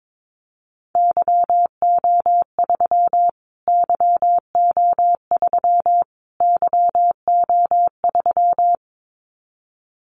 Morse code, heard as a keyed tone, about 22 wpm, 700 Hz. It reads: YO3 YO3 YO3
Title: Callsign Prefixes for Top 25 DXCC - Repeated 3x - 22wpm